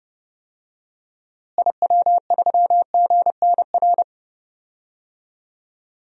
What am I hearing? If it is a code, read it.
Morse code, heard as a keyed tone, about 30 wpm, 700 Hz. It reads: IW3GNR
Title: DXCC-100 Call Signs - 6 Characters - Repeated 2x - 30wpm